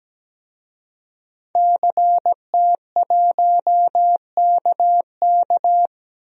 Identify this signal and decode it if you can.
Morse code, heard as a keyed tone, about 17 wpm, 700 Hz. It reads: CT1KK
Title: DXCC-100 Call Signs - 5 Characters - Repeated 3x - 17wpm